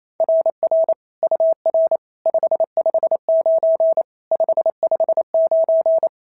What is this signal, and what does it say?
Morse code, heard as a keyed tone, about 28 wpm, 690 Hz. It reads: RR UR 559 559
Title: SOTA QSO Practice 28wpm